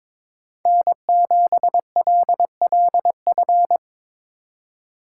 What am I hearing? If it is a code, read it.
Morse code, heard as a keyed tone, about 25 wpm, 700 Hz. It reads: N7LLF
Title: Speed Racing - USA 1x3 Call Signs 22wpm